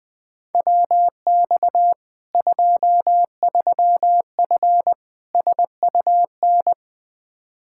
Morse code, heard as a keyed tone, about 20 wpm, 700 Hz. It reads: WX 23F SUN
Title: Practice QSO - Exchange 2 - Info - 20wpm